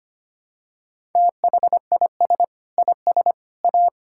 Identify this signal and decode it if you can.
Morse code, heard as a keyed tone, about 25 wpm, 700 Hz. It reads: THIS IS A
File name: practice / 25wpm / Sentences with 3-Word Phrases - 25wpm